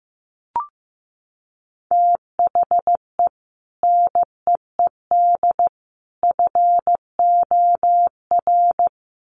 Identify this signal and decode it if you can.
Morse code, heard as a keyed tone, about 15 wpm, 700 Hz. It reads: THE NEED FOR
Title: Sets of 3 Words 15wpm